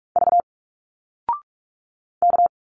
Morse code, heard as a keyed tone, about 44 wpm, 700 Hz. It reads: V K
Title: Single Letters 45wpm